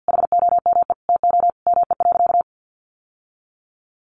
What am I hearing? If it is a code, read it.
Morse code, heard as a keyed tone, about 50 wpm, 700 Hz. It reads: HOME TO ME.
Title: Sentences from Top 1100 Words 50wpm